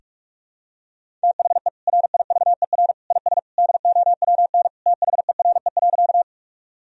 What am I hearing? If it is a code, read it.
Morse code, heard as a keyed tone, about 42 wpm, 700 Hz. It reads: THE RIVER IS DOWN THERE.
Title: Sentences from Top 1000 Words - Review Part 1 - 45wpm